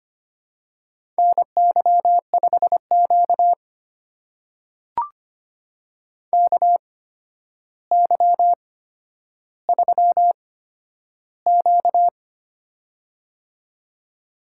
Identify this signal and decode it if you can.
Morse code, heard as a keyed tone, about 25 wpm, spaced as for 11 wpm, 700 Hz. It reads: NY5Q KY3Q
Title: USA Call Signs - 4 Characters - 8x Character Spacing - 25wpm